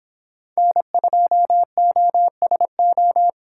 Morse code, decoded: N2OSO

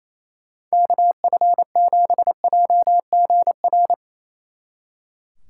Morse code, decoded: KF7JGR

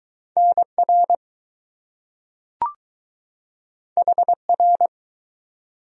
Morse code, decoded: NR HR

